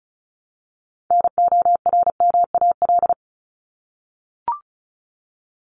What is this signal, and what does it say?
Morse code, heard as a keyed tone, about 34 wpm, 700 Hz. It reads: NORMAL
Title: Top 1001-1250 Words 35wpm